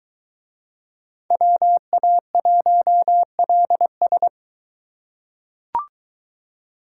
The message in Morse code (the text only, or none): WA1LS